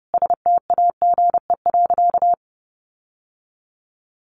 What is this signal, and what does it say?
Morse code, heard as a keyed tone, about 30 wpm, 700 Hz. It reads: STAGE.
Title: Sentences from Top 1200 Words 30wpm